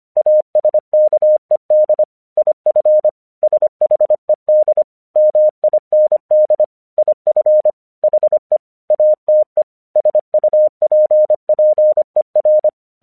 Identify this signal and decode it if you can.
Morse code, heard as a keyed tone, about 25 wpm, 610 Hz. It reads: ASKED IF SHED MIND IF HE ATE SUPPER
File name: 1 English 25wpm / AQC27